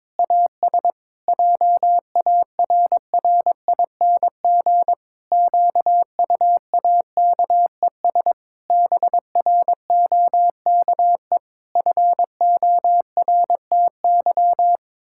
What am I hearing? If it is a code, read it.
Morse code, heard as a keyed tone, about 22 wpm, 700 Hz. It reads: AS JARRING QUAKES BROKE FORTY
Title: Pangram Challenge v2 - 22wpm